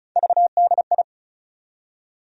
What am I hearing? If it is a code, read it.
Morse code, heard as a keyed tone, about 36 wpm, 700 Hz. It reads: VDI
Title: USA Call Sign Postfixes - 3 Letters 35wpm